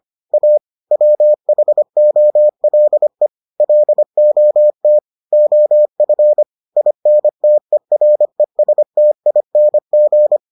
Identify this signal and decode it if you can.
Morse code, heard as a keyed tone, about 25 wpm, 600 Hz. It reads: A WHOLE LOT OF INTERESTING